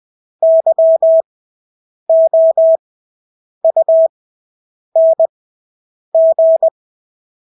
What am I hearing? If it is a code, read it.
Morse code, heard as a keyed tone, about 20 wpm, spaced as for 8 wpm, 650 Hz. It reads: YOUNG
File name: Engelske ord 20/8 wpm